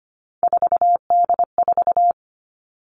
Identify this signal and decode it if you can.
Morse code, heard as a keyed tone, about 25 wpm, 700 Hz. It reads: 4D4
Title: All Cut Numbers - 3-Digit 25wpm